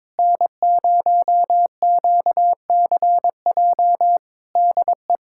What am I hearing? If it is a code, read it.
Morse code, heard as a keyed tone, about 22 wpm, 700 Hz. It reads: N0QCJ DE
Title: Practice QSO - Exchange 4 - Ending - 22wpm